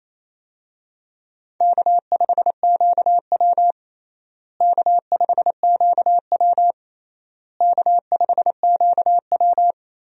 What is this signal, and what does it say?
Morse code, heard as a keyed tone, about 28 wpm, 700 Hz. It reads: K5QW K5QW K5QW
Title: USA Call Signs - 4 Characters - Repeated 3x - 28wpm